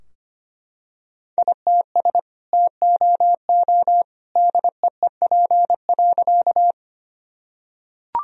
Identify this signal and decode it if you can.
Morse code, heard as a keyed tone, about 25 wpm, 700 Hz. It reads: ITS TOO DEEP.
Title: Sentences from Top 1000 Words - Review Part 3 - 25wpm